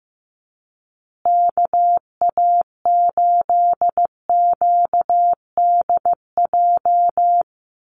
Morse code, heard as a keyed tone, about 15 wpm, 700 Hz. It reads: KA8QDJ
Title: USA Call Signs Repeated 3x 15wpm